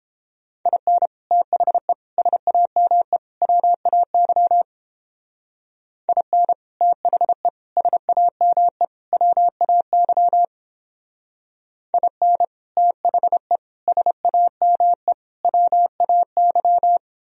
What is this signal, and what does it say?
Morse code, heard as a keyed tone, about 31 wpm, 700 Hz. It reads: IN THE SAME WAY IN THE SAME WAY IN THE SAME WAY